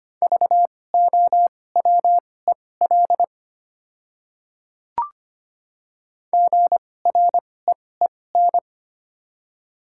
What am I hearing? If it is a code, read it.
Morse code, heard as a keyed tone, about 25 wpm, spaced as for 18 wpm, 700 Hz. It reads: VOWEL GREEN
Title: 2x Character Spacing - Top 500 Words - 25wpm